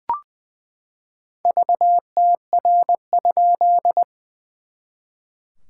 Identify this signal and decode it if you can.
Morse code, heard as a keyed tone, about 20 wpm, 700 Hz. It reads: VTR?
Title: SOTA QSO Practice 20wpm